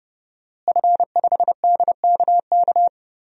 Morse code, heard as a keyed tone, about 30 wpm, 700 Hz. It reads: F5DKK
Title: DXCC-100 Call Signs - 5 Characters - Repeated 3x - 30wpm